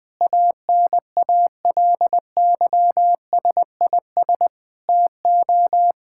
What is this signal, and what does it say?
Morse code, heard as a keyed tone, about 20 wpm, 700 Hz. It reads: ANALYSIS TO
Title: Sentences from Top 1000 Words - Review Part 5 - 20wpm